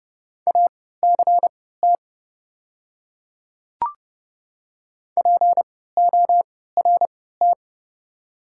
Morse code, heard as a keyed tone, about 30 wpm, spaced as for 17 wpm, 700 Hz. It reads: ACT PORT
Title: 3x Character Spacing - Top 200 Words - 30wpm